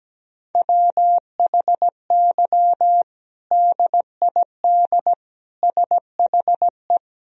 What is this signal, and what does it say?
Morse code, heard as a keyed tone, about 17 wpm, 700 Hz. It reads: WHY DID SHE
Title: Sentences from Top 1000 Words - Review Part 3 - 17wpm